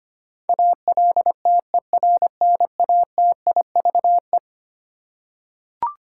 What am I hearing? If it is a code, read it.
Morse code, heard as a keyed tone, about 25 wpm, 700 Hz. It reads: ALTERNATIVE
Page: morse-code-ninja-practice-25wpm